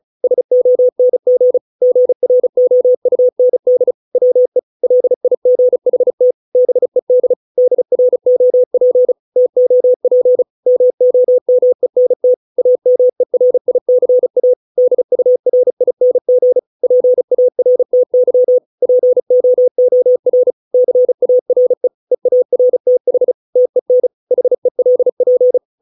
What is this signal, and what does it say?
Morse code, heard as a keyed tone, about 34 wpm, 500 Hz. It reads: SONG GROUND WE LIGHT BED DROP TOP MOMENT AMERICA DURING PARTY POOR CARE EARTH TEN HELP